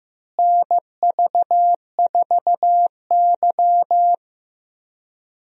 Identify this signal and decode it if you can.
Morse code, heard as a keyed tone, about 15 wpm, 700 Hz. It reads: NV4Y
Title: USA Call Signs - 4 Characters - Repeated 3x - 15wpm